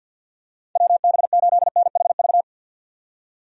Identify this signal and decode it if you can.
Morse code, heard as a keyed tone, about 51 wpm, 700 Hz. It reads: WB8NHV
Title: USA Call Signs - 6 Characters - 50wpm